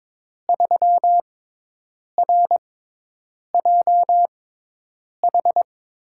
Morse code, heard as a keyed tone, about 22 wpm, spaced as for 8 wpm, 700 Hz. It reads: 3RJH
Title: USA Call Signs - 5 Characters - 6x Character Spacing - 22wpm